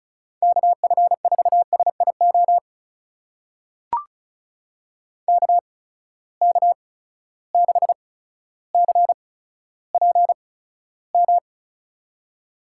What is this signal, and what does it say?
Morse code, heard as a keyed tone, about 36 wpm, spaced as for 15 wpm, 700 Hz. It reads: KF4SIO KK6CPM
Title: USA Call Signs - 6 Characters - 8x Character Spacing - 35wpm